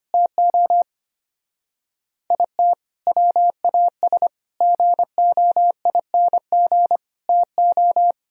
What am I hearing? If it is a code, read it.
Morse code, heard as a keyed tone, about 30 wpm, 700 Hz. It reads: TO IT WAS GOING TO
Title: Speed-Racing - Sets of 4 Words 25wpm